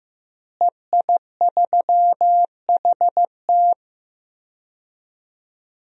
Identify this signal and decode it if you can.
Morse code, heard as a keyed tone, about 15 wpm, 700 Hz. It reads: EI3HT
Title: DXCC-100 Call Signs - 5 Characters - Repeated 3x - 15wpm